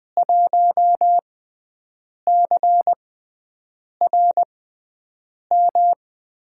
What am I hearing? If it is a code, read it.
Morse code, heard as a keyed tone, about 20 wpm, spaced as for 7 wpm, 700 Hz. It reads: 1CRM
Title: USA Call Signs - 5 Characters - 6x Character Spacing - 20wpm